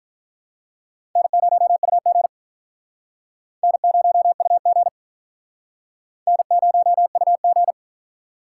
Morse code, heard as a keyed tone, about 45 wpm, 700 Hz. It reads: N0UG N0UG N0UG